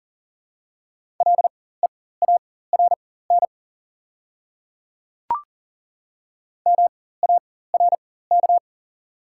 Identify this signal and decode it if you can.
Morse code, heard as a keyed tone, about 40 wpm, spaced as for 19 wpm, 700 Hz. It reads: LEARN MARK
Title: 4x Character Spacing - Top 300 Words - 40wpm